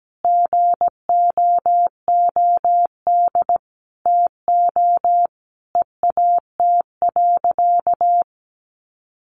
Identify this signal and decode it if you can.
Morse code, heard as a keyed tone, about 17 wpm, 700 Hz. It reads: GOOD TO EAT.
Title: Sentences from Top 800 Words 17wpm